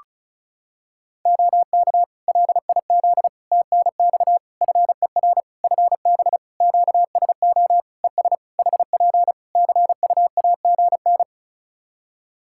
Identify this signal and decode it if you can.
Morse code, heard as a keyed tone, about 36 wpm, 700 Hz. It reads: OK LIZ TNX FER FB QSO ES HP CUAGN
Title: Practice QSO - Exchange 4 - Ending - 35wpm